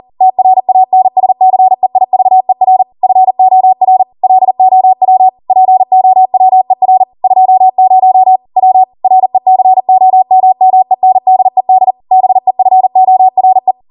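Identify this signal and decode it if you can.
Morse code, heard as a keyed tone, about 40 wpm, 750 Hz. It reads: TRANSCEIVER FOR LOW POWER 20 W RECOMMENDED BEFORE
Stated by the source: ARRL 40 WPM Morse code practice transmission.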